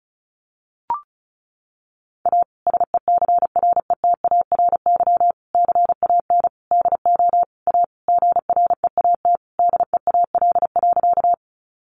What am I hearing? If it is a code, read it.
Morse code, heard as a keyed tone, about 34 wpm, 700 Hz. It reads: A SECRETARY CAN DO A GREAT DEAL.
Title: Sentences from Top 1300 Words 35wpm